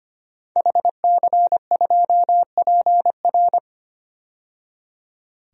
Morse code, heard as a keyed tone, about 25 wpm, 700 Hz. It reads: HC2PR
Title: DXCC-100 Call Signs - 5 Characters - 25wpm